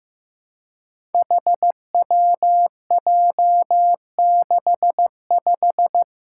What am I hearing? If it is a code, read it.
Morse code, heard as a keyed tone, about 15 wpm, 700 Hz. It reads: HWJ65
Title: Five Challenging Letter-Number Combinations 15wpm